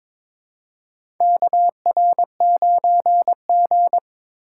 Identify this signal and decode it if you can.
Morse code, heard as a keyed tone, about 22 wpm, 700 Hz. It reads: KR9G
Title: USA Call Signs - 4 Characters - 2x Character Spacing - 22wpm